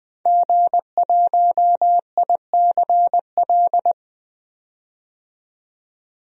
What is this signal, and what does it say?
Morse code, heard as a keyed tone, about 20 wpm, 700 Hz. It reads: G1ICL
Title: DXCC-100 Call Signs - 5 Characters - 20wpm